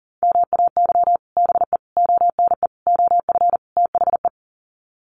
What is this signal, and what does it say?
Morse code, heard as a keyed tone, about 40 wpm, 700 Hz. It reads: MAY BE ONE OF THE
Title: Sets of 5 Words Encore2 40wpm